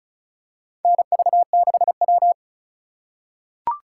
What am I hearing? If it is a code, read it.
Morse code, heard as a keyed tone, about 36 wpm, 700 Hz. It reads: NV6W
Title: USA Call Signs - 4 Characters - 4x Character Spacing - 35wpm